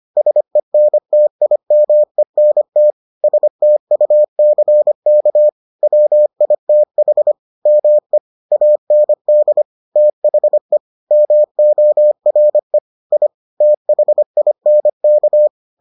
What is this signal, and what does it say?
Morse code, heard as a keyed tone, about 25 wpm, 600 Hz. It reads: SENTIMENT STUCK WITH ME AND THE MORE I THINK